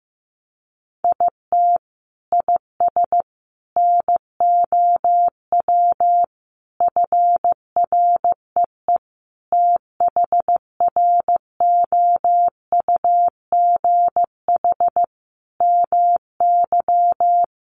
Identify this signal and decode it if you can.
Morse code, heard as a keyed tone, about 15 wpm, 700 Hz. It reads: IT IS NOW FREE THROUGH MY
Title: Sentences from Top 1300 Words 15wpm